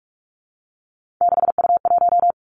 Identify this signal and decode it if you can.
Morse code, heard as a keyed tone, about 46 wpm, 700 Hz. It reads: 6U1
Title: Three Challenging Letter-Number Combinations 45wpm